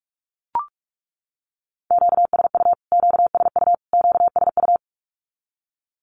Morse code, heard as a keyed tone, about 44 wpm, 700 Hz. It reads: QSU QSU QSU
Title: Q-Codes Repeated 3x 45wpm